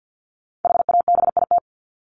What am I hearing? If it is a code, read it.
Morse code, heard as a keyed tone, about 48 wpm, 700 Hz. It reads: HABIT
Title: Top 1751-2000 Words 50wpm